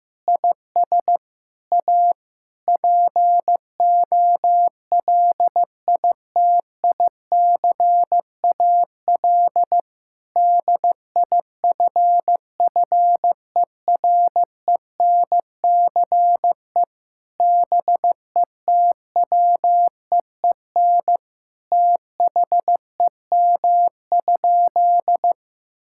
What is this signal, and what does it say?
Morse code, heard as a keyed tone, about 15 wpm, 700 Hz. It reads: IS A POLITICAL DIFFERENCE BETWEEN THEM?
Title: Sentences from Top 1000 Words - Review Part 5 - 15wpm